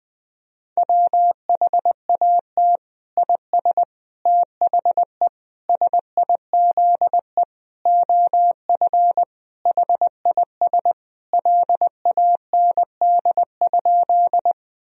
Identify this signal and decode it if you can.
Morse code, heard as a keyed tone, about 20 wpm, 700 Hz. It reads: WHAT IS THE SIZE OF HIS LAND?
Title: Sentences from Top 1000 Words - Review Part 2 - 20wpm